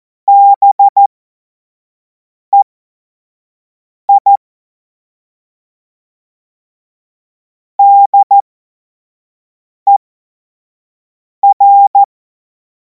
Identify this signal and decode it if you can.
Morse code, heard as a keyed tone, about 14 wpm, spaced as for 5 wpm, 800 Hz. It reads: BEI DER
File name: Morsekurs / Witze / Witz 10. März 2026 25 F70